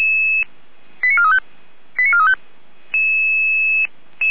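Five-tone selective call (ZVEI), heard as quick runs of five tones, tones of about 70 ms. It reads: 89435 89435
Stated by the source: ZVEI five-tone selective call received on SDR, fire brigade alerting.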